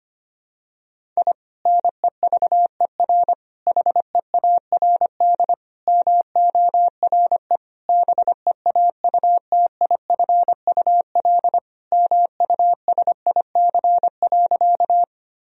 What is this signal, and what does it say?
Morse code, heard as a keyed tone, about 25 wpm, 700 Hz. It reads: I NEVER HEARD MORE BEAUTIFUL MUSIC.
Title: Sentences from Top 1000 Words - Review Part 1 - 25wpm